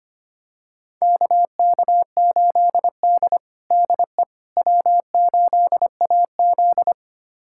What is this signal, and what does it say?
Morse code, heard as a keyed tone, about 25 wpm, 700 Hz. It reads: KK8D DE W8AZ
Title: USA Callsign DE Callsign - 4 Characters - 25wpm